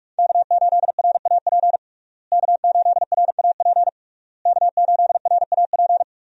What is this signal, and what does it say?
Morse code, heard as a keyed tone, about 44 wpm, 700 Hz. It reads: K8RAP K8RAP K8RAP
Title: USA Call Signs - 5 Characters - Repeated 3x - 45wpm